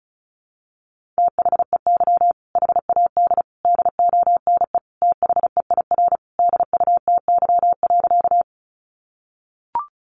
Morse code, heard as a keyed tone, about 34 wpm, 700 Hz. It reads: THEY HAD DONE THEIR DUTY.
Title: Sentences from Top 1500 Words 35wpm